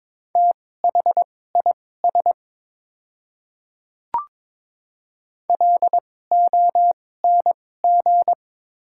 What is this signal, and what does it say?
Morse code, heard as a keyed tone, about 22 wpm, spaced as for 16 wpm, 690 Hz. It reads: THIS LONG
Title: 2x Character Spacing - Top 100 Words - 22wpm